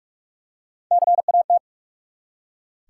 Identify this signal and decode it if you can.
Morse code, heard as a keyed tone, about 44 wpm, 700 Hz. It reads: CAT
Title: Three Letter Words - 45wpm